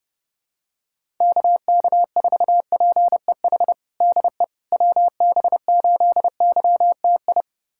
Morse code, heard as a keyed tone, about 30 wpm, 700 Hz. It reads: KK4PEH DE WB8YTI
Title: USA Callsign DE Callsign - 6 Characters - 30wpm